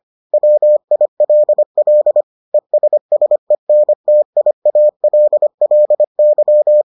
Morse code, heard as a keyed tone, about 25 wpm, 600 Hz. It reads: WILL ESSENTIALLY